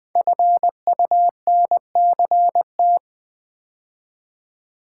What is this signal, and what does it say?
Morse code, heard as a keyed tone, about 20 wpm, 700 Hz. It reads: FUNCT
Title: Build A Word - 5 Letters At A Time - 10-Letter Words - 20wpm